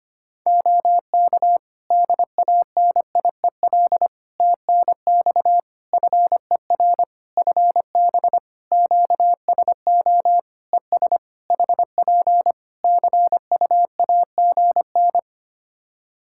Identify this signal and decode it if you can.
Morse code, heard as a keyed tone, about 25 wpm, 700 Hz. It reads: OK DANIEL TNX FER FB QSO ES HP CUAGN
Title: K1USN SST QSO Practice 25wpm